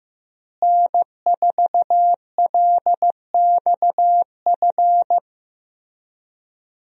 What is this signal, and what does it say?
Morse code, heard as a keyed tone, about 15 wpm, 700 Hz. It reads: N4LXF
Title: USA Call Signs - 5 Characters - Repeated 2x - 15wpm